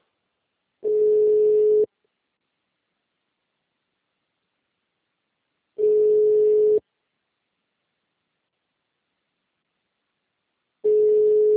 Ringback tone (Europe, 425 Hz).